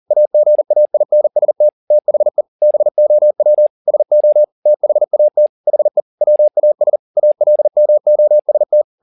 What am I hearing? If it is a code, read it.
Morse code, heard as a keyed tone, about 40 wpm, 600 Hz. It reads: AGAINST THE BOW SO THAT HE WAS ALMOST